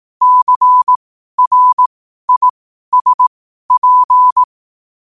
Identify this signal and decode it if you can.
Morse code, heard as a keyed tone, about 18 wpm, spaced as for 12 wpm, 1000 Hz. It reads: CRISP
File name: qotdfiles / 12wpm